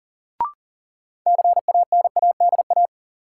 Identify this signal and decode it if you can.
Morse code, heard as a keyed tone, about 40 wpm, 700 Hz. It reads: CANADA
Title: Sets of 5 Words - New Word Drill - 40wpm